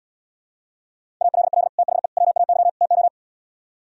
Morse code, heard as a keyed tone, about 77 wpm, 700 Hz. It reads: ALL THE WAY TO